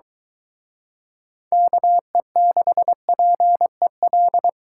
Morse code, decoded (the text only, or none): KE6PEL